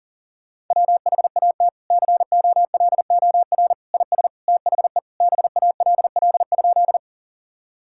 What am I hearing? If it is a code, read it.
Morse code, heard as a keyed tone, about 40 wpm, 700 Hz. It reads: WHAT COLOR IS THE BALL?
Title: Sentences from Top 1000 Words - Review Part 6 - 40wpm